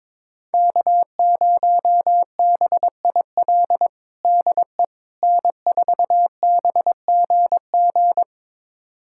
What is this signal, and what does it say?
Morse code, heard as a keyed tone, about 22 wpm, 700 Hz. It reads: K0BIL DE N4BGG
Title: USA Callsign DE Callsign - 5 Characters - 22wpm